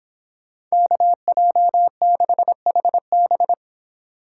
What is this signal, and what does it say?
Morse code, heard as a keyed tone, about 26 wpm, 700 Hz. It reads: KJ6HB